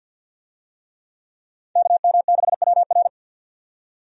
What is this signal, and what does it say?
Morse code, heard as a keyed tone, about 52 wpm, 700 Hz. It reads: KM6WR
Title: USA Call Signs - 5 Characters - 50wpm